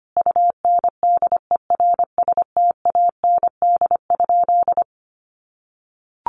Morse code, heard as a keyed tone, about 25 wpm, 700 Hz. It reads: UNDERSTAND?
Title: Sentences from Top 1500 Words 25wpm